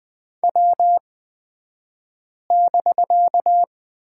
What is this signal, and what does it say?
Morse code, heard as a keyed tone, about 20 wpm, spaced as for 5 wpm, 700 Hz. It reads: W<BK>
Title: 254 Lesson 41d - Period - Two Characters 20wpm